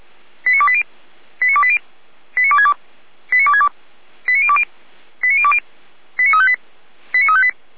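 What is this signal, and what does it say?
Five-tone selective call (ZVEI), heard as quick runs of five tones, tones of about 70 ms. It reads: 89280 89280 89262 89262 89020 89020 89358 89358
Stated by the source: Five-tone ZVEI selective call 89320 received by SDR.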